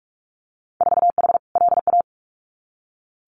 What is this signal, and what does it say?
Morse code, heard as a keyed tone, about 46 wpm, 690 Hz. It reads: 4H LA
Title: Winter Field Day Exchanges Encore 45wpm